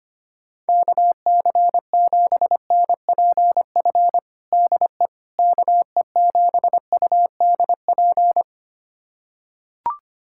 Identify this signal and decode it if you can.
Morse code, heard as a keyed tone, about 25 wpm, 700 Hz. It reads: KC7NPF DE KE7UDP
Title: USA Callsign DE Callsign - 6 Characters - 25wpm